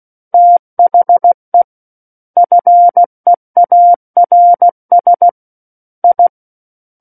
Morse code, heard as a keyed tone, about 16 wpm, 700 Hz. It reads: THE FEARS I
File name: Book One / 07-Chapter 7 - 16 WPM